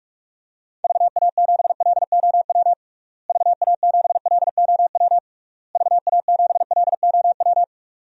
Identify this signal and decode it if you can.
Morse code, heard as a keyed tone, about 41 wpm, 700 Hz. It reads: VA7LOW VA7LOW VA7LOW
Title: DXCC-100 Call Signs - 6 Characters - Repeated 3x - 45wpm